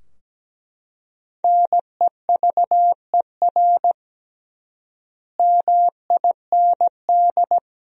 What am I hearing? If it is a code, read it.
Morse code, heard as a keyed tone, about 17 wpm, 700 Hz. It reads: NEVER MIND
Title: 3x Word-Spacing - Sentences from Top 500 Words - 17wpm